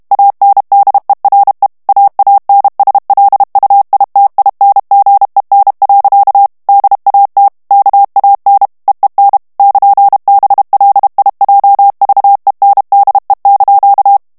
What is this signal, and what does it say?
Morse code, heard as a keyed tone, about 32 wpm, 800 Hz. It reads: ANDERE AANSLUITINGEN. DAT KAN EEN <KN>BLIJVENDE)
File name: morse mp3 / HOLLANDAIS texte clair 2 vitesse 30 mots minute